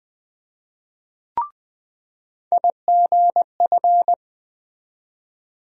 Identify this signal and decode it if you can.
Morse code, heard as a keyed tone, about 20 wpm, 700 Hz. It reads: IGF
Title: USA Call Sign Postfixes - 3 Letters 20wpm